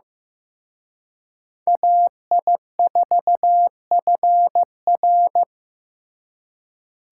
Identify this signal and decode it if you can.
Morse code, heard as a keyed tone, about 18 wpm, 700 Hz. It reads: AI4FR
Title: Speed Racing - USA Call Signs 15wpm